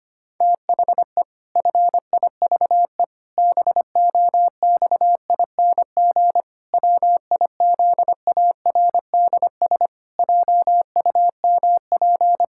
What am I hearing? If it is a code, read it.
Morse code, heard as a keyed tone, about 25 wpm, 700 Hz. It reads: THE FIVE BOXING WIZARDS JUMP
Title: Pangram Challenge v2 - 25wpm